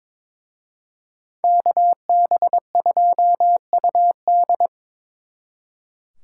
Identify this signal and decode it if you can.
Morse code, heard as a keyed tone, about 26 wpm, 700 Hz. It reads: KB2UD
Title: Speed Racing - USA Call Signs 22wpm